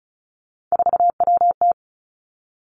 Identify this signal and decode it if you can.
Morse code, heard as a keyed tone, about 36 wpm, 700 Hz. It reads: 4WT
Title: Three Challenging Letter-Number Combinations 35wpm